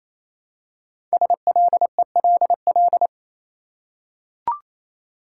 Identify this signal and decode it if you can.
Morse code, heard as a keyed tone, about 29 wpm, 700 Hz. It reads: SLELL
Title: Code Groups - 5 Letters - 28wpm